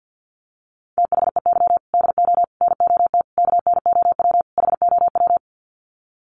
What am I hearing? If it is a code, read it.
Morse code, heard as a keyed tone, about 48 wpm, 700 Hz. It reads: THEY DO NOT KNOW HOW
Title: Sets of 5 Words Encore2 50wpm